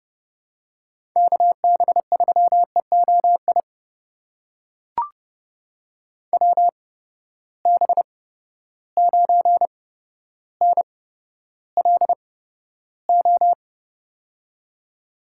Morse code, heard as a keyed tone, about 30 wpm, spaced as for 13 wpm, 700 Hz. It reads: KB3EOI WB9NLO
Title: USA Call Signs - 6 Characters - 8x Character Spacing - 30wpm